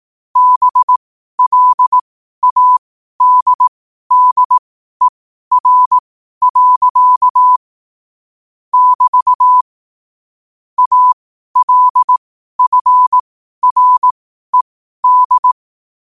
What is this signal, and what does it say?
Morse code, heard as a keyed tone, about 18 wpm, spaced as for 12 wpm, 1000 Hz. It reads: BLADDER. <BT> ALFRED
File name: qotdfiles / 12wpm